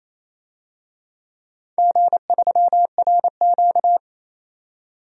Morse code, heard as a keyed tone, about 28 wpm, 700 Hz. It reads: G3RQ
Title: DXCC-100 Call Signs - 4 Characters - 28wpm